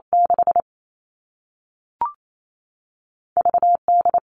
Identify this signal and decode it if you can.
Morse code, heard as a keyed tone, about 28 wpm, spaced as for 3 wpm, 700 Hz. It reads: 6V D
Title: All Cut Numbers - 2-Digit 28wpm